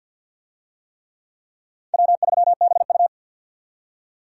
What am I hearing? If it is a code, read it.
Morse code, heard as a keyed tone, about 47 wpm, 700 Hz. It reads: W3BU
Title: USA Call Signs - 4 Characters - 50wpm